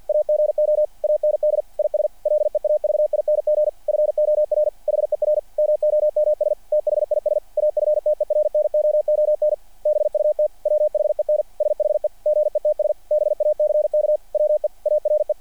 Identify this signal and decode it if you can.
Morse code, heard as a keyed tone, about 48 wpm, 600 Hz. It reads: AGO AND IS LEAVING FOR HER MOMS THIS AFTERNOON BUT WHEN SHE GETS BACK WE ARE
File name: archived / CRSnet